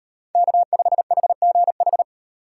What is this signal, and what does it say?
Morse code, heard as a keyed tone, about 38 wpm, 700 Hz. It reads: K5HGH